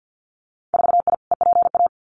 Morse code, heard as a keyed tone, about 50 wpm, 700 Hz. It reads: 4I EPA
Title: Winter Field Day Exchanges 50wpm